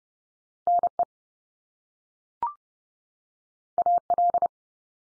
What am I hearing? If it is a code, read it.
Morse code, heard as a keyed tone, about 30 wpm, 700 Hz. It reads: NE AL
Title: US States 30WPM